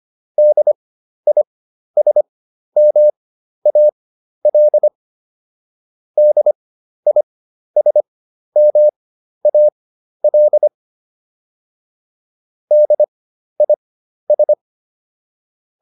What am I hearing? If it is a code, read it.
Morse code, heard as a keyed tone, about 25 wpm, spaced as for 12 wpm, 600 Hz. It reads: DISMAL DISMAL DIS